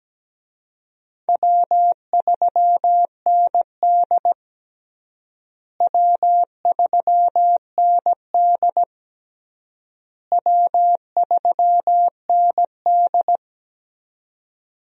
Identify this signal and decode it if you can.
Morse code, heard as a keyed tone, about 17 wpm, 700 Hz. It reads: W3ND W3ND W3ND
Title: USA Call Signs - 4 Characters - Repeated 3x - 17wpm